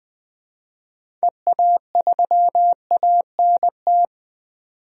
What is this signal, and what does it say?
Morse code, heard as a keyed tone, about 21 wpm, 700 Hz. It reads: EA3ANT
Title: DXCC-100 Call Signs - 6 Characters - 20wpm